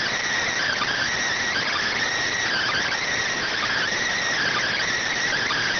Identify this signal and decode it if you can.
Emergency Alert System (SAME) tones (header bursts), heard as harsh buzzing bursts. It is heard from 0.5 to 4.8 s.
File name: findings / CT SAGITAL 27.gen-18 rx